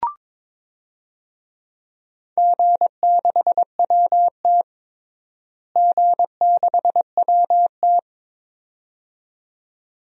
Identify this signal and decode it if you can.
Morse code, heard as a keyed tone, about 22 wpm, 700 Hz. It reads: G6WT G6WT
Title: DXCC-100 Call Signs - 4 Characters - Repeated 2x - 22wpm